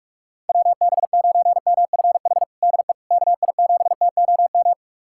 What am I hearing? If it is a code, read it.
Morse code, heard as a keyed tone, about 44 wpm, 700 Hz. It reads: WB9MFH DE KI7TOM
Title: USA Callsign DE Callsign - 6 Characters - 45wpm